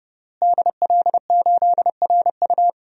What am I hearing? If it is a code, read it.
Morse code, heard as a keyed tone, about 32 wpm, 700 Hz. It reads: DL8RU
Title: DXCC-100 Call Signs - 5 Characters - Repeated 2x - 30wpm